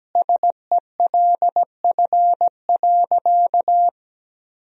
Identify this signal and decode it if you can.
Morse code, heard as a keyed tone, about 17 wpm, 700 Hz. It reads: SELF.
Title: Sentences from Top 1000 Words - Review Part 1 - 17wpm